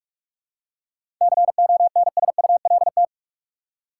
Morse code, heard as a keyed tone, about 46 wpm, 700 Hz. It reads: CONSULT
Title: Seven Letter Words - 45wpm